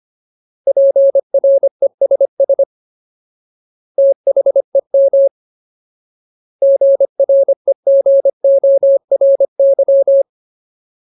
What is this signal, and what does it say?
Morse code, heard as a keyed tone, about 25 wpm, 550 Hz. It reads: PRESS THEM GREGORY
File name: Ultimate Word List / 25WPM Ultimate